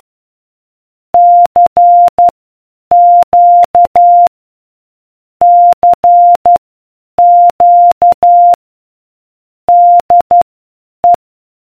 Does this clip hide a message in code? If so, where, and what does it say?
Yes: Morse code from 1.1 to 11.1 s, heard as a keyed tone, about 12 wpm, spaced as for 8 wpm, 700 Hz. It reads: CQ CQ DE